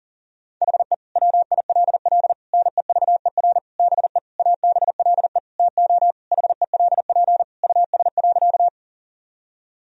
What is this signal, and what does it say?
Morse code, heard as a keyed tone, about 40 wpm, 700 Hz. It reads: HE WILL NEVER BE ABLE TO HELP US.
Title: Sentences from Top 1000 Words - Review Part 2 - 40wpm